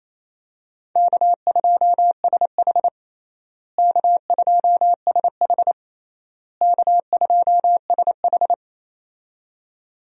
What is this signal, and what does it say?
Morse code, heard as a keyed tone, about 28 wpm, 700 Hz. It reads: K2SH K2SH K2SH
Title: USA Call Signs - 4 Characters - Repeated 3x - 28wpm